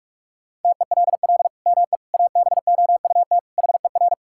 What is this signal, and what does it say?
Morse code, heard as a keyed tone, about 46 wpm, 700 Hz. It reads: TELL ME ABOUT HER
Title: Sentences from Top 1000 Words - Review Part 5 - 45wpm